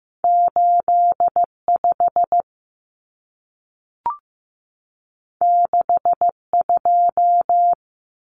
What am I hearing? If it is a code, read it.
Morse code, heard as a keyed tone, about 15 wpm, 700 Hz. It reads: 8 5 6 2
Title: Numbers - 2-Digit 15wpm